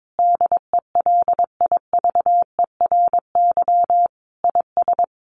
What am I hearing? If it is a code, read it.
Morse code, heard as a keyed tone, about 22 wpm, 700 Hz. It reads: DELIVERY IS
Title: Sentences from Top 1500 Words 22wpm